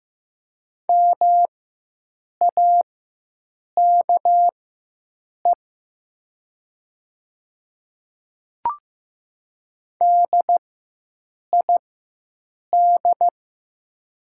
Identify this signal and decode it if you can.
Morse code, heard as a keyed tone, about 15 wpm, spaced as for 7 wpm, 700 Hz. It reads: MAKE DID
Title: 4x Character Spacing - Top 200 Words - 15wpm